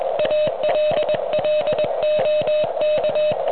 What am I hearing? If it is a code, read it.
Morse code, heard as a keyed tone, about 22 wpm, 610 Hz, with noise. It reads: ALL OK